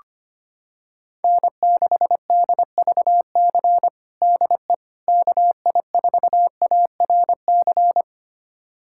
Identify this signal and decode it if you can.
Morse code, heard as a keyed tone, about 25 wpm, 700 Hz. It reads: N6DVC DE KI4ARC
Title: USA Callsign DE Callsign 25wpm